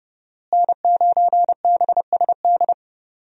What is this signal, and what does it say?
Morse code, heard as a keyed tone, about 30 wpm, 700 Hz. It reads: N9BSD
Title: USA Call Signs - 5 Characters - Repeated 2x - 30wpm